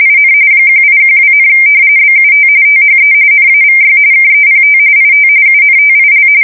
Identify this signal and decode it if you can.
RTTY, about 45.45 baud. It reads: IN 1812, BUT THE NAVY KEPT IT UNTIL
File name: fsk170 / usnavy2